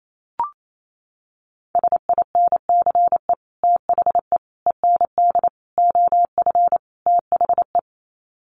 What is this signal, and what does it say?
Morse code, heard as a keyed tone, about 28 wpm, 700 Hz. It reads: SINCE THE END OF THE
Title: Sets of 5 Words Encore 28wpm